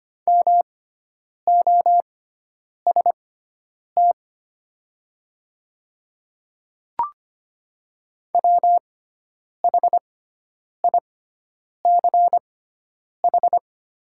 Morse code, heard as a keyed tone, about 25 wpm, spaced as for 9 wpm, 700 Hz. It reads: MOST WHICH
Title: 6x Character Spacing - Top 300 Words - 25wpm